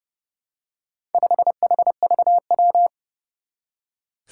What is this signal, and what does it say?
Morse code, heard as a keyed tone, about 30 wpm, 700 Hz. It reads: 5HVW
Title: Four Challenging Letter-Number Combinations 30wpm